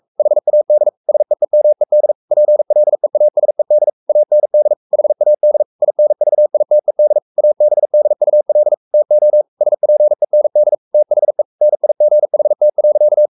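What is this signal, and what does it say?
Morse code, heard as a keyed tone, about 44 wpm, 600 Hz. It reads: HAD SEEMED PLEASED AND HAD INVITED ABDUL TO SPEND THE NIGHT.
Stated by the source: Morse code at 43 wpm sending Ken Follett's novel Never in English, from CW med Gnister.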